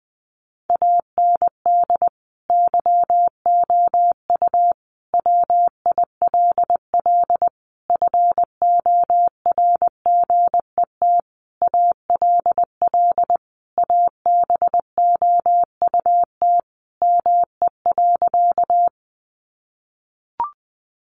Morse code, heard as a keyed tone, about 20 wpm, 700 Hz. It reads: AND YOU WILL FORGET ALL ABOUT ME.
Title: Sentences from Top 1200 Words 20wpm